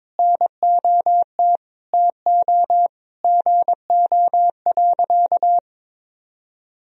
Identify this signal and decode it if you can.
Morse code, heard as a keyed tone, about 22 wpm, 700 Hz. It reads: NOT TO GO.
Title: Sentences from Top 1000 Words - Review Part 4 - 22wpm